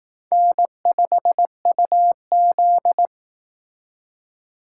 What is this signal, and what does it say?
Morse code, heard as a keyed tone, about 19 wpm, 700 Hz. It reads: N5UZ
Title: Speed Racing - USA 1x2 Call Signs 15wpm